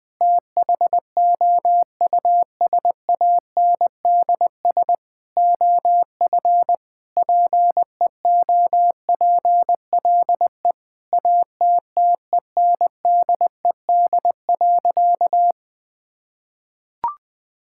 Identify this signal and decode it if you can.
Morse code, heard as a keyed tone, about 20 wpm, 700 Hz. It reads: THOUSANDS OF PEOPLE ATTENDED.
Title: Sentences with 4-Word Phrases - 20wpm